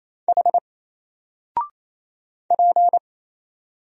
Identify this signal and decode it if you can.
Morse code, heard as a keyed tone, about 27 wpm, 700 Hz. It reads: H P
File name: Single-Letter-Number-28wpm